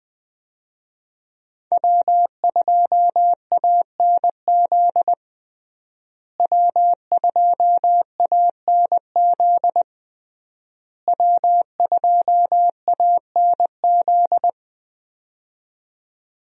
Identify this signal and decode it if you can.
Morse code, heard as a keyed tone, about 20 wpm, 700 Hz. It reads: W2ANZ W2ANZ W2ANZ
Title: USA Call Signs - 5 Characters - Repeated 3x - 20wpm